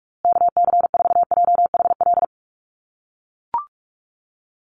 Morse code, heard as a keyed tone, about 44 wpm, 700 Hz. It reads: KC4JHL
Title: USA Call Signs Repeated 2x 45wpm